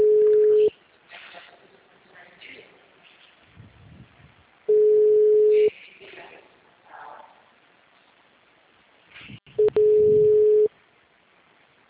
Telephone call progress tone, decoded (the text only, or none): ringback tone